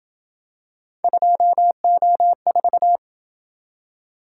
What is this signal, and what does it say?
Morse code, heard as a keyed tone, about 27 wpm, 700 Hz. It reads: 2O4